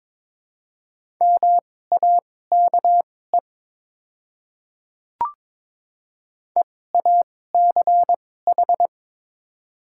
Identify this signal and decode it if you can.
Morse code, heard as a keyed tone, about 22 wpm, spaced as for 16 wpm, 700 Hz. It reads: MAKE EACH
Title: 2x Character Spacing - Top 300 Words - 22wpm